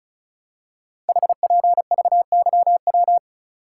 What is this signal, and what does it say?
Morse code, heard as a keyed tone, about 36 wpm, 700 Hz. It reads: HPVYW
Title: Code Groups - 5 Letters - 35wpm